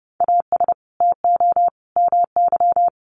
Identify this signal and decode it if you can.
Morse code, heard as a keyed tone, about 30 wpm, 700 Hz. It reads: AS TO MY
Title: Sentences from Top 600 Words 30wpm